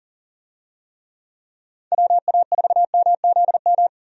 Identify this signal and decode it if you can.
Morse code, heard as a keyed tone, about 40 wpm, 700 Hz. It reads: WA4MZM
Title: USA Call Signs - 6 Characters - 40wpm